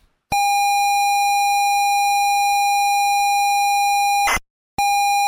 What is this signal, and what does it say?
ZX Spectrum tape loading.